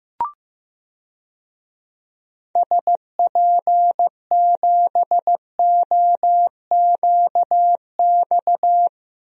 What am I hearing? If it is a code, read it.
Morse code, heard as a keyed tone, about 15 wpm, 700 Hz. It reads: SP7OQX
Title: DXCC-100 Call Signs - 6 Characters - Repeated 2x - 15wpm